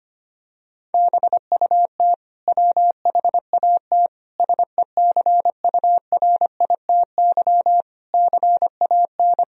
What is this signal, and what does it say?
Morse code, heard as a keyed tone, about 25 wpm, 700 Hz. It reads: BUT WHAT SECURITY CAN
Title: Sentences from Top 1000 Words - Review Part 4 - 25wpm